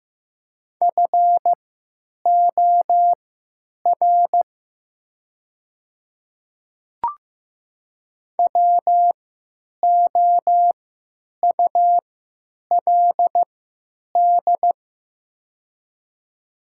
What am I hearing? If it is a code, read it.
Morse code, heard as a keyed tone, about 15 wpm, spaced as for 9 wpm, 690 Hz. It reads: FOR WOULD
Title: 3x Character Spacing - Top 100 Words - 15wpm